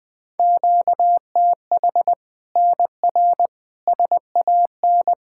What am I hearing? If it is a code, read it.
Morse code, heard as a keyed tone, about 21 wpm, 700 Hz. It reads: QTH NR SAN
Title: Practice QSO - Exchange 1 - Beginning - 20wpm